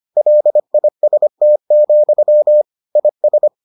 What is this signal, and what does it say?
Morse code, heard as a keyed tone, about 25 wpm, 600 Hz. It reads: LIST, IS